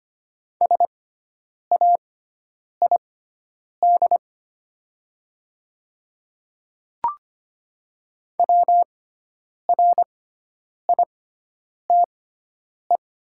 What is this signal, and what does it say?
Morse code, heard as a keyed tone, about 25 wpm, spaced as for 9 wpm, 700 Hz. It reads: SAID WRITE
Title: 6x Character Spacing - Top 100 Words - 25wpm